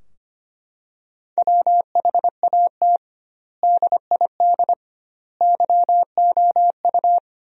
Morse code, heard as a keyed tone, about 25 wpm, 700 Hz. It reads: WHAT DID YOU
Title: 2x Word-Spacing - Sentences from Top 100 Words - 25wpm